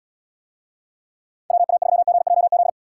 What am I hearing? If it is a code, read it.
Morse code, heard as a keyed tone, about 78 wpm, 700 Hz. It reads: WA3GJZ